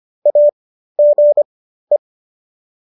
Morse code, heard as a keyed tone, about 25 wpm, spaced as for 13 wpm, 590 Hz. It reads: AGE